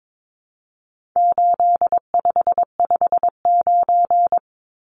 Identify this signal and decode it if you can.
Morse code, heard as a keyed tone, about 22 wpm, 700 Hz. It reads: 8559
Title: Numbers - 4-Digit 22wpm